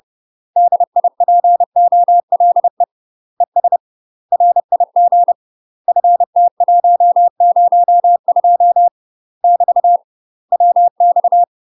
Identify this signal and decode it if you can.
Morse code, heard as a keyed tone, about 30 wpm, 700 Hz. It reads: DIPOLE ES RIG FT102 <BT> WX
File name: UK MASTER QSOs TRAINING / 30WPM / UK MASTER QSOs 0005